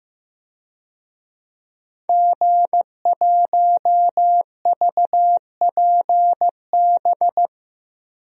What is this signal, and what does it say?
Morse code, heard as a keyed tone, about 15 wpm, 700 Hz. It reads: G1VPB
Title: DXCC-100 Call Signs - 5 Characters - Repeated 3x - 15wpm